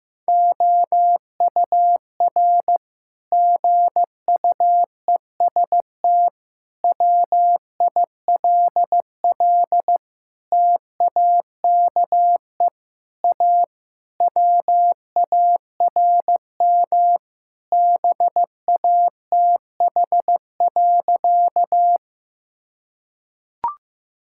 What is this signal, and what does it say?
Morse code, heard as a keyed tone, about 15 wpm, 700 Hz. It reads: OUR GUEST WILL TAKE A WARM BATH.
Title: Sentences from Top 1600 Words 15wpm